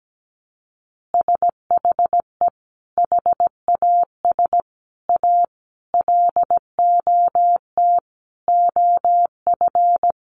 Morse code, heard as a keyed tone, about 17 wpm, 700 Hz. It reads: SHE HAS A LOT OF
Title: Additional Easy Phrases 17wpm